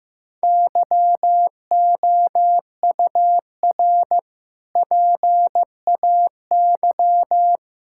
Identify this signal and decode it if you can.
Morse code, heard as a keyed tone, about 15 wpm, 700 Hz. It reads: YOUR PAY
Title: Sentences from Top 1000 Words - Review Part 6 - 15wpm